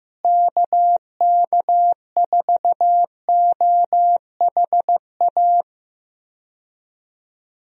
Morse code, heard as a keyed tone, about 15 wpm, 700 Hz. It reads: KK4OHA
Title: USA Call Signs - 6 Characters - Repeated 3x - 15wpm